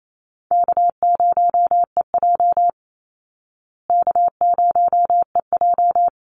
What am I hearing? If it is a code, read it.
Morse code, heard as a keyed tone, about 28 wpm, 700 Hz. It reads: K0EJ K0EJ
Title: Call Signs Repeated 2x - 28wpm